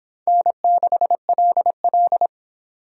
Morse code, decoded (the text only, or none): N6LL